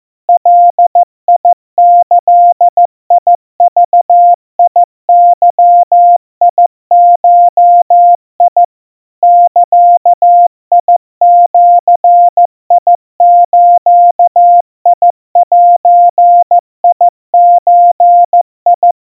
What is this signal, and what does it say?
Morse code (Japanese Wabun code), heard as a keyed tone, about 14 wpm, 700 Hz. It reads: ガギグゲゴ ザジズゼゾ
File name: kawa / Morse sample / aiueo 50 cpm 700 Hz Crispy